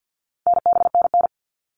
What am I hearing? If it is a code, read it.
Morse code, heard as a keyed tone, about 50 wpm, 700 Hz. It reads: NBNN
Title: All Cut Numbers - 4-Digit 50wpm